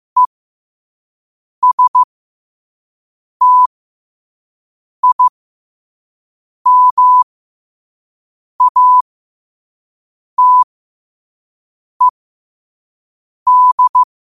Morse code, heard as a keyed tone, about 15 wpm, spaced as for 5 wpm, 1000 Hz. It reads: ESTIMATED
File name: 5wpm